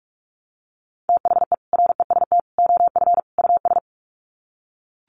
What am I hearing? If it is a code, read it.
Morse code, heard as a keyed tone, about 44 wpm, 700 Hz. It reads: THE REST OF US
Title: Sets of 4 Words 45wpm